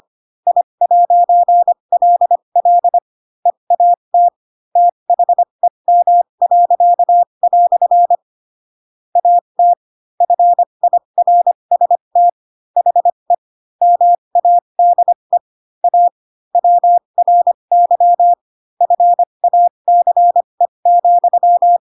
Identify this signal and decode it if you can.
Morse code, heard as a keyed tone, about 25 wpm, spaced as for 22 wpm, 700 Hz. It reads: I'LL EAT THEM." AT FIRST HE MADE A WRY FACE,
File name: Pinocchio / Pinocchio ch7 18 25 ik1hge